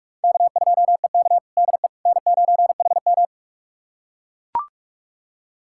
Morse code, heard as a keyed tone, about 44 wpm, 700 Hz. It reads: K2EK DE N9HM
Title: USA Callsign DE Callsign - 4 Characters - 45wpm